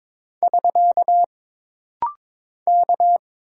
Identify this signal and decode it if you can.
Morse code, heard as a keyed tone, about 22 wpm, 700 Hz. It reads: <SK> K
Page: This rapid-fire timing is geared to push your ICR (Instant Character Recognition) skills.